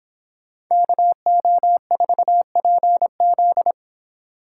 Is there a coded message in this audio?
Yes: Morse code from 0.7 to 3.7 s, heard as a keyed tone, about 30 wpm, 700 Hz. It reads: KO4PZ